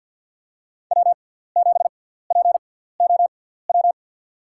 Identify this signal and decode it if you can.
Morse code, heard as a keyed tone, about 48 wpm, spaced as for 17 wpm, 700 Hz. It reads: W7POW
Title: USA Call Signs - 5 Characters - 6x Character Spacing - 50wpm